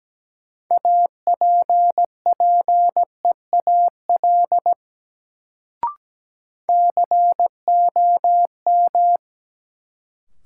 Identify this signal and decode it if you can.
Morse code, heard as a keyed tone, about 17 wpm, 700 Hz. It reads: APPEAL COM
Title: Sentences from Top 1300 Words - New Word Drill - 17wpm